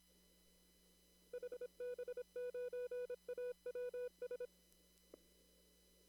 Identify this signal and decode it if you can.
Morse code, heard as a keyed tone, about 26 wpm, 490 Hz. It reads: HB9AWS
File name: CW / kt5j wpx cw-op-k5pi